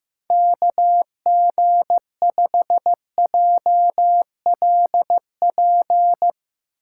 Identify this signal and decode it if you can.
Morse code, heard as a keyed tone, about 15 wpm, 700 Hz. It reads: KG5JLP
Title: USA Call Signs - 6 Characters - 2x Character Spacing - 15wpm